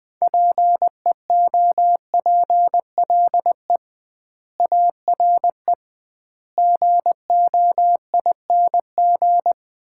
Morse code, heard as a keyed tone, about 20 wpm, 700 Hz. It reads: PEOPLE ARE GOING
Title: 2x Word-Spacing - Sets of 4 Words - 20wpm